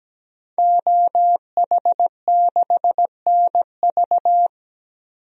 Morse code, heard as a keyed tone, about 17 wpm, 700 Hz. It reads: OH6NV
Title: DXCC-100 Call Signs - 5 Characters - 17wpm